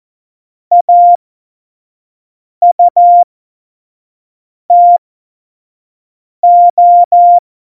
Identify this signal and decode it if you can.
Morse code, heard as a keyed tone, about 14 wpm, spaced as for 5 wpm, 700 Hz. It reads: AUTO
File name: Morsekurs / Witze / Witz 28 Juli 2020, 25 F70